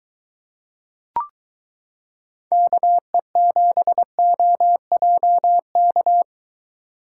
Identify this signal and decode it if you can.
Morse code, heard as a keyed tone, about 22 wpm, 700 Hz. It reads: KE7OJK